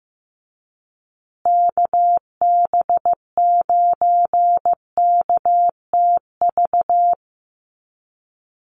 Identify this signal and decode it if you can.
Morse code, heard as a keyed tone, about 15 wpm, 700 Hz. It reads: KB9KTV
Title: USA Call Signs Repeated 2x 15wpm